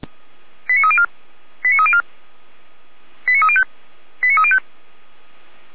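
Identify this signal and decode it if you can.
Five-tone selective call (ZVEI), heard as quick runs of five tones, tones of about 70 ms. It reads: 89304 89304 89305 89305
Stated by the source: ZVEI five-tone selective call received via SDR.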